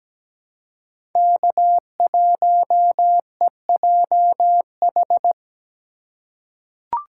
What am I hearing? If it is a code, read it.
Morse code, heard as a keyed tone, about 17 wpm, 700 Hz. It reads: K1EJH
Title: USA Call Signs - 5 Characters - 2x Character Spacing - 17wpm